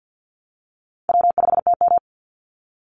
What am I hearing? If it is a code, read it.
Morse code, heard as a keyed tone, about 49 wpm, 700 Hz. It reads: W5TM
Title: Call Signs Repeated 3x - 50wpm